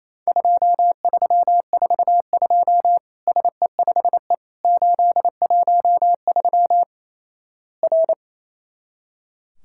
Morse code, heard as a keyed tone, about 28 wpm, 700 Hz. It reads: 2342 SE5E 813 R
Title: Worked All Europe DX Contest - QTC Exchange - 28wpm